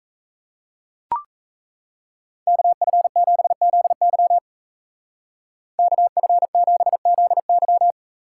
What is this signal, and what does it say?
Morse code, heard as a keyed tone, about 37 wpm, 700 Hz. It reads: KF7ZY KF7ZY